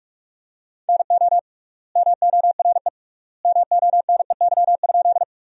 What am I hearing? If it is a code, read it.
Morse code, heard as a keyed tone, about 46 wpm, 700 Hz. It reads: NO MORE MONEY?
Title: 3x Word-Spacing - Sentences from Top 400 Words - 45wpm